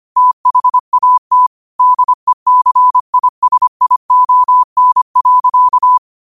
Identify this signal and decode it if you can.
Morse code, heard as a keyed tone, about 25 wpm, 1000 Hz. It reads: THAT DECISION.
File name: qotdfiles / 25wpm